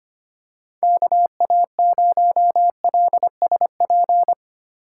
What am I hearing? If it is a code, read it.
Morse code, heard as a keyed tone, about 25 wpm, 700 Hz. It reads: KA0LSP
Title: USA Call Signs - 6 Characters - 4x Character Spacing - 25wpm